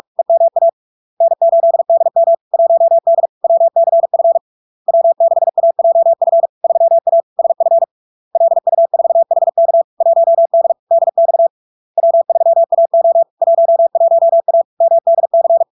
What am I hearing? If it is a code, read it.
Morse code, heard as a keyed tone, about 46 wpm, 670 Hz. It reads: EMA N8DM 1D WCF W6AJF 3A SF LU4HK 1D DX W3AO 11A MDC
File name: LARK-FD-2022-Log-in-CW-45wpm-670-Hz